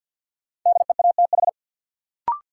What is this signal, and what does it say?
Morse code, heard as a keyed tone, about 52 wpm, 700 Hz. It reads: DEATH
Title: Sentences from Top 1100 Words - New Word Drill Alt - 50wpm